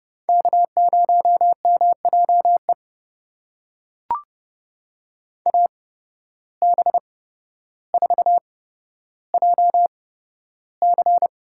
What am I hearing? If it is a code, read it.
Morse code, heard as a keyed tone, about 30 wpm, spaced as for 13 wpm, 700 Hz. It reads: K0MJE AB4JC
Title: USA Call Signs - 5 Characters - 8x Character Spacing - 30wpm